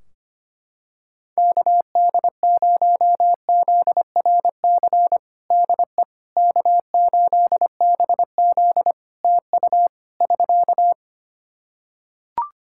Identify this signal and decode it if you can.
Morse code, heard as a keyed tone, about 25 wpm, 700 Hz. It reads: KD0ZRC DE K8BZ TU <SK>
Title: Practice QSO - Exchange 4 - Ending - 25wpm